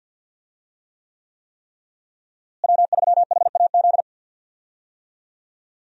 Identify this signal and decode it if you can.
Morse code, heard as a keyed tone, about 51 wpm, 700 Hz. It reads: W3HAZ
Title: USA Call Signs - 5 Characters - Repeated 3x - 50wpm